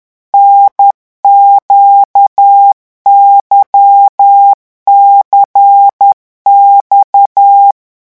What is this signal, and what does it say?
Morse code, heard as a keyed tone, about 10 wpm, 800 Hz. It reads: NQYCX
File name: morse mp3 / revision3-vitesse 10 mots